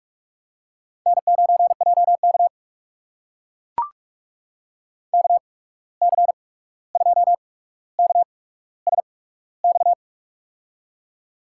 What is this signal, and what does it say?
Morse code, heard as a keyed tone, about 44 wpm, spaced as for 17 wpm, 700 Hz. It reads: N9JK KC2KSX
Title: USA Call Signs - 8x Character Spacing - 45wpm